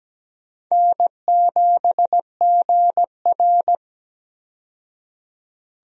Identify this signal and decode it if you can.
Morse code, heard as a keyed tone, about 17 wpm, 700 Hz. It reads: N7GR
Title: USA Call Signs - 4 Characters - 17wpm